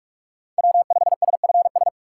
Morse code, heard as a keyed tone, about 46 wpm, 700 Hz. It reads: W5SFS
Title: USA Call Signs - 5 Characters - 2x Character Spacing - 45wpm